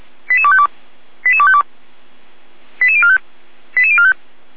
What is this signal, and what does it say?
Five-tone selective call (ZVEI), heard as quick runs of five tones, tones of about 70 ms. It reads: 88252 88252 88045 88045